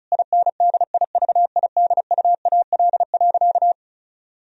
Morse code, heard as a keyed tone, about 36 wpm, 700 Hz. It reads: INDIVIDUAL.
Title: Sentences from Top 1000 Words - Review Part 4 - 35wpm